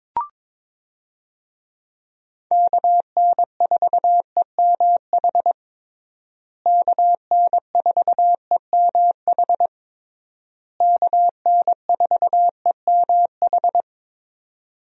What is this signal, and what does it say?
Morse code, heard as a keyed tone, about 22 wpm, 700 Hz. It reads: KN4EMH KN4EMH KN4EMH
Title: USA Call Signs - 6 Characters - Repeated 3x - 22wpm